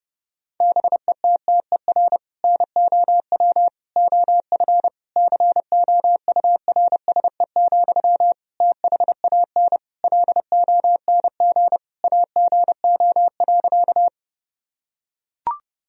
Morse code, heard as a keyed tone, about 32 wpm, 700 Hz. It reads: BETTER NOW OF COURSE, THAN LONG AGO.
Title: Sentences from Top 1000 Words - Review Part 3 - 30wpm